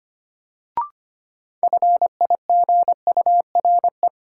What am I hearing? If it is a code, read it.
Morse code, heard as a keyed tone, about 25 wpm, 700 Hz. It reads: FIGURE
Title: Sets of 4 Words - New Word Drill - 25wpm